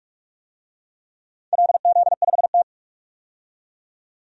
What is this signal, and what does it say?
Morse code, heard as a keyed tone, about 46 wpm, 700 Hz. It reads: LZ5T
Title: DXCC-100 Call Signs - 4 Characters - 45wpm